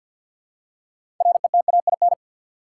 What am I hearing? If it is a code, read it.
Morse code, heard as a keyed tone, about 47 wpm, 700 Hz. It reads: RETAIN
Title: Six Letter Words - 50wpm